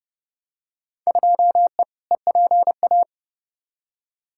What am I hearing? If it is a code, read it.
Morse code, heard as a keyed tone, about 30 wpm, 700 Hz. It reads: 2E EPA
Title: Field Day Exchanges Encore - 30wpm